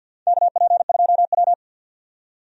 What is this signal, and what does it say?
Morse code, heard as a keyed tone, about 53 wpm, 700 Hz. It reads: KP2W
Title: DXCC-100 Call Signs - 4 Characters - 50wpm